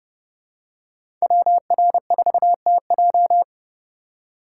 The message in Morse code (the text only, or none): WR4TJ